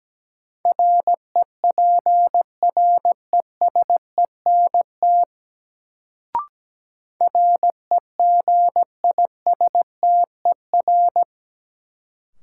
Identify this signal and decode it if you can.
Morse code, heard as a keyed tone, about 17 wpm, 700 Hz. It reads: REPRESENT REGISTER
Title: Sentences from Top 1200 Words - New Word Drill Alt - 17wpm